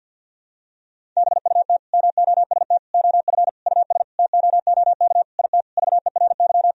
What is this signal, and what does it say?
Morse code, heard as a keyed tone, about 51 wpm, 700 Hz. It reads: BUT MOST OF US TOOK IT VERY
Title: Sentences from Top 1000 Words - Review Part 1 - 50wpm